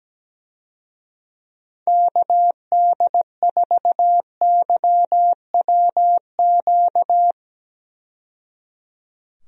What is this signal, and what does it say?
Morse code, heard as a keyed tone, about 17 wpm, 700 Hz. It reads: KD4YWQ
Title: USA Call Signs - 6 Characters - 17wpm